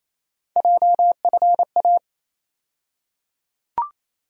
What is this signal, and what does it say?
Morse code, heard as a keyed tone, about 28 wpm, 700 Hz. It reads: JFA
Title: USA Call Sign Postfixes - 3 Letters 28wpm